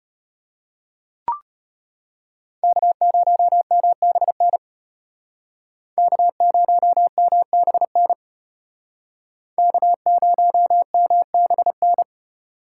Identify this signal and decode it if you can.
Morse code, heard as a keyed tone, about 33 wpm, 700 Hz. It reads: K0MBN K0MBN K0MBN